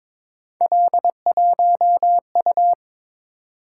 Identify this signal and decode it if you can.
Morse code, heard as a keyed tone, about 22 wpm, 700 Hz. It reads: L1U
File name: practice / 22wpm / Three Letter-Number Combinations - 22wpm